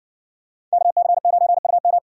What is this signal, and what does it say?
Morse code, heard as a keyed tone, about 60 wpm, 700 Hz. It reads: KC9UM